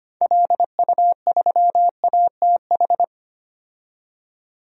Morse code, heard as a keyed tone, about 25 wpm, 700 Hz. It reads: LU3ATH
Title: DXCC-100 Call Signs - 6 Characters - 25wpm